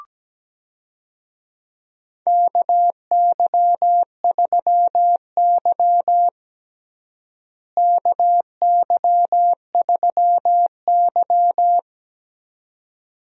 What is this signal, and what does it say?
Morse code, heard as a keyed tone, about 17 wpm, 700 Hz. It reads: KY3Y KY3Y
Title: USA Call Signs - 4 Characters - Repeated 2x - 17wpm